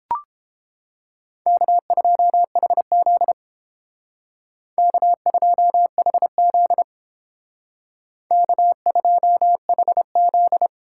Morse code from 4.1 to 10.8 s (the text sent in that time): K2HZ K2HZ